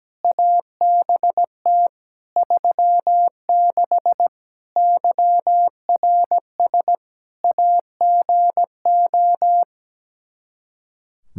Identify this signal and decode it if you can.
Morse code, heard as a keyed tone, about 17 wpm, 700 Hz. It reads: ABT 36 YRS AGO
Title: Practice QSO - Exchange 3 - More Info - 17wpm